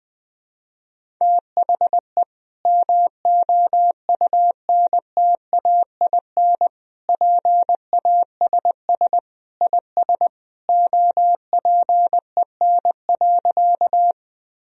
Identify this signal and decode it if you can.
Morse code, heard as a keyed tone, about 21 wpm, 700 Hz. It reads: THE MOUNTAIN PASS IS OPEN.
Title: Sentences from Top 1000 Words - Review Part 1 - 20wpm